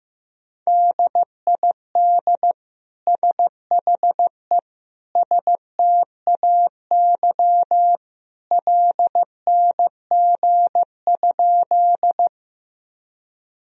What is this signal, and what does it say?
Morse code, heard as a keyed tone, about 15 wpm, 700 Hz. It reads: DID SHE STAY LNG?
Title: Sentences from Top 500 Words and Abbreviations 15wpm